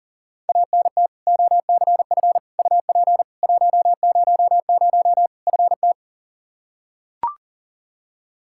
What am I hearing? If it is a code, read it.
Morse code, heard as a keyed tone, about 40 wpm, 700 Hz. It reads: ANT OCF UP 100 FT
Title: K1USN SST QSO Practice 40wpm